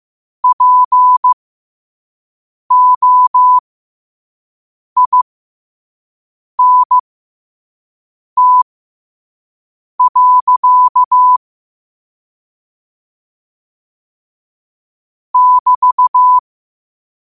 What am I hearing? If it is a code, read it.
Morse code, heard as a keyed tone, about 15 wpm, spaced as for 5 wpm, 1000 Hz. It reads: POINT. <BT>
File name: qotdfiles / 5wpm